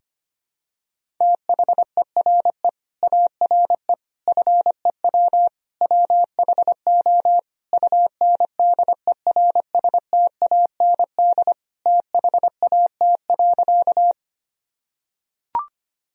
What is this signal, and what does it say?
Morse code, heard as a keyed tone, about 25 wpm, 700 Hz. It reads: THERE ARE FEW WHO UNDERSTAND THAT.
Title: Sentences from Top 1000 Words - Review Part 1 - 25wpm